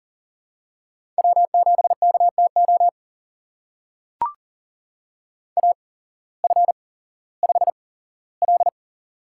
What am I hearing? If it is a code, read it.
Morse code, heard as a keyed tone, about 42 wpm, spaced as for 19 wpm, 700 Hz. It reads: W7KTO AF5L
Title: USA Call Signs - 8x Character Spacing - 40wpm